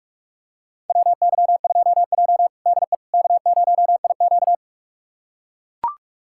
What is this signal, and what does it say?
Morse code, heard as a keyed tone, about 44 wpm, 690 Hz. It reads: WY2J DE K0IQ
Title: USA Callsign DE Callsign - 4 Characters - 45wpm